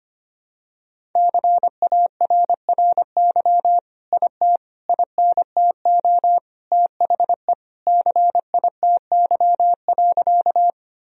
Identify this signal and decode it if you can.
Morse code, heard as a keyed tone, about 25 wpm, 700 Hz. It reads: CARRY IT INTO THE CITY.
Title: Sentences from Top 1000 Words - Review Part 1 - 25wpm